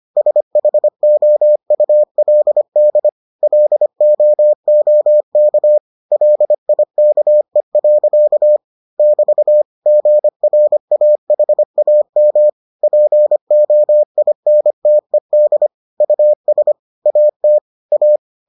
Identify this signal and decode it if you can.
Morse code, heard as a keyed tone, about 25 wpm, 600 Hz. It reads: SHOULD LOOK LIKE. <BT> GRAHAM POINTED US AT A